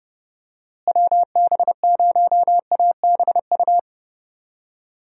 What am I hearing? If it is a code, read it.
Morse code, heard as a keyed tone, about 31 wpm, 700 Hz. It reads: WB0ABU
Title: USA Call Signs - 6 Characters - Repeated 3x - 30wpm